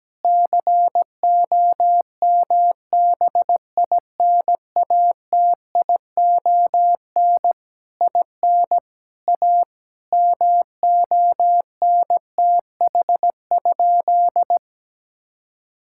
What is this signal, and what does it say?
Morse code, heard as a keyed tone, about 17 wpm, 700 Hz. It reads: COMBINATION IN A MONTH?
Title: Sentences from Top 1000 Words - Review Part 6 - 17wpm